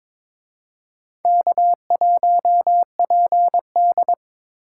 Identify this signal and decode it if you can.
Morse code, heard as a keyed tone, about 26 wpm, 700 Hz. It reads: K1PD
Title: Speed Racing - USA Call Signs 22wpm